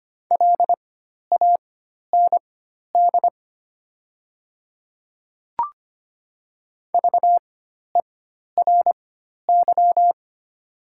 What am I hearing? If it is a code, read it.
Morse code, heard as a keyed tone, about 25 wpm, spaced as for 12 wpm, 700 Hz. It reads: LAND VERY
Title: 4x Character Spacing - Top 200 Words - 25wpm